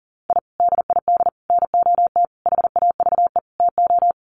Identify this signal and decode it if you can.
Morse code, heard as a keyed tone, about 40 wpm, 700 Hz. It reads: I DID NOT HAVE TO
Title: Sets of 5 Words 40wpm